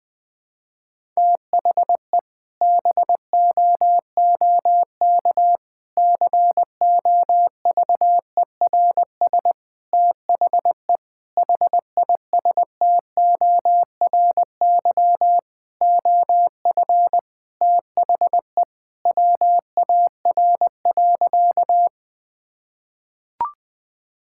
Morse code, heard as a keyed tone, about 20 wpm, 700 Hz. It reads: THE BOOK COVERS THE HISTORY OF THE WAR.
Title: Sentences with 4-Word Phrases - Spoken Beginning - 20wpm